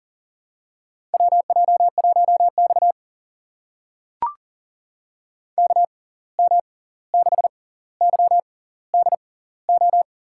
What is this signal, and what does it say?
Morse code, heard as a keyed tone, about 40 wpm, spaced as for 18 wpm, 700 Hz. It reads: WJ1X KM6YDO
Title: USA Call Signs - 6x Character Spacing - 40wpm